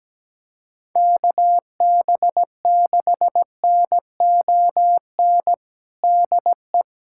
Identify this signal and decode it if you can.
Morse code, heard as a keyed tone, about 17 wpm, 700 Hz. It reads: KB6NON DE
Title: USA Callsign DE Callsign 17wpm